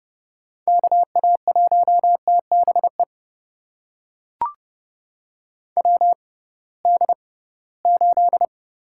Morse code, heard as a keyed tone, about 30 wpm, spaced as for 19 wpm, 700 Hz. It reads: KA1TBE WD8
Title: USA Call Signs - 6 Characters - 6x Character Spacing - 30wpm